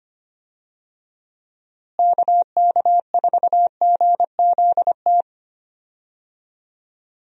Morse code, heard as a keyed tone, about 25 wpm, 700 Hz. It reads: KK4GZT
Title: USA Call Signs - 6 Characters - Repeated 3x - 25wpm